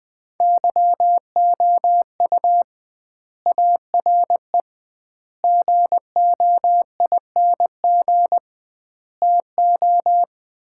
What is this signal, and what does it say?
Morse code, heard as a keyed tone, about 20 wpm, 700 Hz. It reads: YOU ARE GOING TO
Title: 2x Word-Spacing - Sets of 4 Words - 20wpm